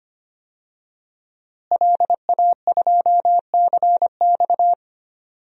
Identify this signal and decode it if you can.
Morse code, heard as a keyed tone, about 25 wpm, 700 Hz. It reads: LA2CX
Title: DXCC-100 Call Signs - 5 Characters - Repeated 2x - 25wpm